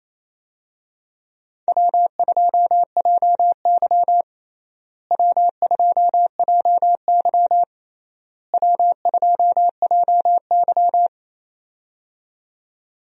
Morse code, heard as a keyed tone, about 27 wpm, 700 Hz. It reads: W2JY W2JY W2JY
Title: USA Call Signs - 4 Characters - Repeated 3x - 28wpm